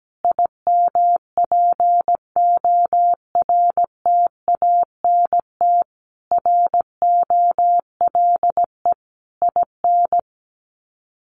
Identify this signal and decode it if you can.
Morse code, heard as a keyed tone, about 17 wpm, 700 Hz. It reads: IMPORTANT ROLE IN
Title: Sets of 5 Words Encore 17wpm